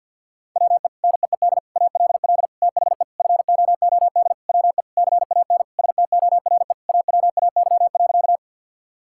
Morse code, heard as a keyed tone, about 50 wpm, 700 Hz. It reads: WE NEED ALL THE FOOD WE CAN STORE AWAY.
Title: Sentences from Top 1000 Words - Review Part 5 - 50wpm